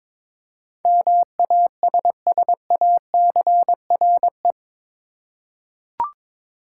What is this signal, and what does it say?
Morse code, heard as a keyed tone, about 22 wpm, 700 Hz. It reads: MASSACRE
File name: practice / 22wpm / Eight Letter Words - 22wpm